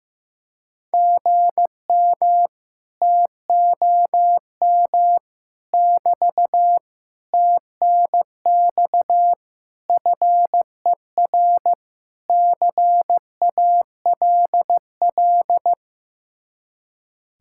Morse code, heard as a keyed tone, about 15 wpm, 700 Hz. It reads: GM TOM <BT> TNX FER CALL
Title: Speed-Racing Common QSO Phrases Encore - 15wpm